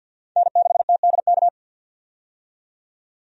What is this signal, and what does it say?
Morse code, heard as a keyed tone, about 51 wpm, 700 Hz. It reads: N6TDK
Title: USA Call Signs - 5 Characters - 50wpm